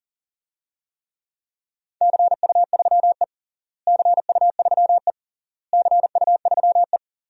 Morse code, heard as a keyed tone, about 37 wpm, 700 Hz. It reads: CU3E CU3E CU3E
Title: DXCC-100 Call Signs - 4 Characters - Repeated 3x - 40wpm